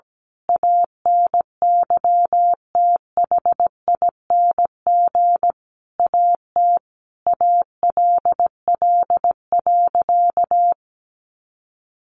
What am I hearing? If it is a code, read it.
Morse code, heard as a keyed tone, about 17 wpm, 700 Hz. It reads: ANYTHING AT ALL.
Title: Sentences from Top 1300 Words 17wpm